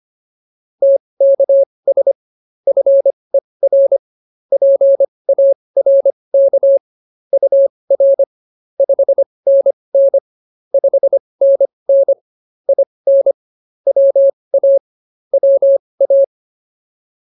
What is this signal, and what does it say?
Morse code, heard as a keyed tone, about 24 wpm, spaced as for 20 wpm, 550 Hz. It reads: TKS FER PARK UR 5NN 5NN IN WA WA